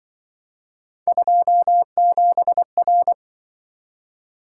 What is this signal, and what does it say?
Morse code, heard as a keyed tone, about 25 wpm, 700 Hz. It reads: 27R